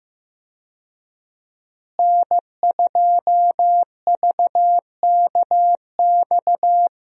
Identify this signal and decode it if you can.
Morse code, heard as a keyed tone, about 15 wpm, 700 Hz. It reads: N2VKX
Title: USA Call Signs - 5 Characters - Repeated 2x - 15wpm